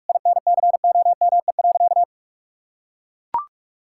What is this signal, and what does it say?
Morse code, heard as a keyed tone, about 41 wpm, 700 Hz. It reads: INCOME.
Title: Sentences from Top 1000 Words - Review Part 5 - 45wpm